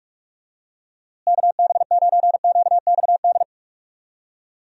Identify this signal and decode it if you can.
Morse code, heard as a keyed tone, about 46 wpm, 700 Hz. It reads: KB9QXD
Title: USA Call Signs - 6 Characters - Repeated 3x - 45wpm